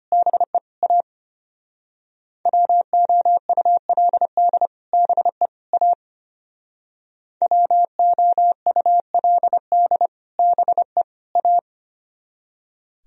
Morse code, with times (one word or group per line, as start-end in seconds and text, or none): 0.1-0.6 BE
0.8-1.0 A
2.5-4.7 WOULD
4.9-5.5 BE
5.7-5.9 A
7.4-10.1 WOULD
10.4-11.0 BE
11.4-11.6 A